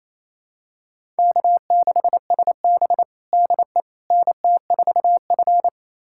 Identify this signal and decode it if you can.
Morse code, heard as a keyed tone, about 28 wpm, 700 Hz. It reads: K6SB DE NT4F
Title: USA Callsign DE Callsign - 4 Characters - 28wpm